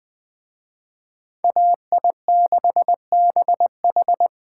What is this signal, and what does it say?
Morse code, heard as a keyed tone, about 20 wpm, 700 Hz. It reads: AI6BH